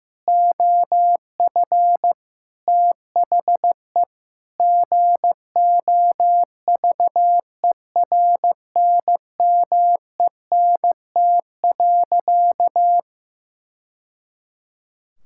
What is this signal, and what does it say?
Morse code, heard as a keyed tone, about 15 wpm, 700 Hz. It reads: OF THE GOVERNMENT.
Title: Sentences from Top 1000 Words - Review Part 5 - 15wpm